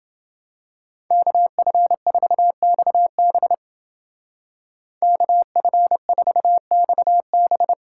Morse code, heard as a keyed tone, about 28 wpm, 700 Hz. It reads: KF4XB KF4XB